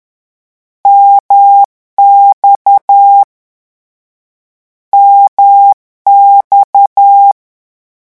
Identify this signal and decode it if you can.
Morse code, heard as a keyed tone, about 10 wpm, 800 Hz. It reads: MX MX
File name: morse mp3 / lecon1-vitesse 10